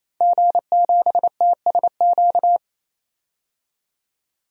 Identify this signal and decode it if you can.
Morse code, heard as a keyed tone, about 28 wpm, 700 Hz. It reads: G7TSQ
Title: DXCC-100 Call Signs - 5 Characters - 28wpm